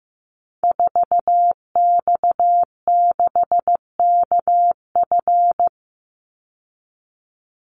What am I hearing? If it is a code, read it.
Morse code, heard as a keyed tone, about 15 wpm, 700 Hz. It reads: 4X6KF
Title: Call Signs Repeated 2x - 15wpm